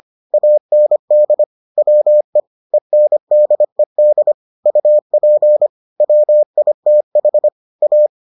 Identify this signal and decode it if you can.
Morse code, heard as a keyed tone, about 25 wpm, 600 Hz. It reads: AND WE ENDED UP WITH A